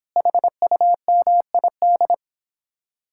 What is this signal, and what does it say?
Morse code, heard as a keyed tone, about 29 wpm, 700 Hz. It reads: HUMID